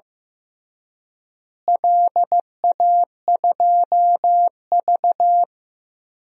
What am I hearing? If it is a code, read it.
Morse code, heard as a keyed tone, about 15 wpm, 700 Hz. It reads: LA2V
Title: DXCC-100 Call Signs - 4 Characters - Repeated 3x - 15wpm